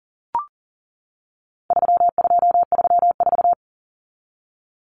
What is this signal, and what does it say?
Morse code, heard as a keyed tone, about 40 wpm, 700 Hz. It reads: 3234
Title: Common Cut Numbers - 4-Digit 40wpm